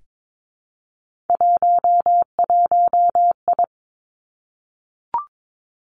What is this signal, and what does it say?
Morse code, heard as a keyed tone, about 22 wpm, 700 Hz. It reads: 11I
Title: Three Challenging Letter-Number Combinations 22wpm